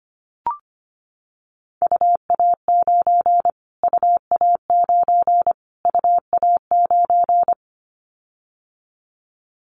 Morse code, heard as a keyed tone, about 25 wpm, 700 Hz. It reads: UA9 UA9 UA9
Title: Callsign Prefixes for Top 25 DXCC - Repeated 3x - 25wpm